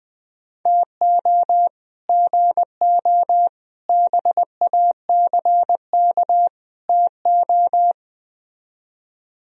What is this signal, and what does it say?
Morse code, heard as a keyed tone, about 20 wpm, 700 Hz. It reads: TO GO BACK TO